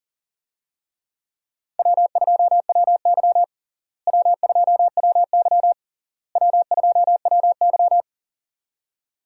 Morse code, heard as a keyed tone, about 40 wpm, 700 Hz. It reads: W2WY W2WY W2WY
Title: USA Call Signs - 4 Characters - Repeated 3x - 40wpm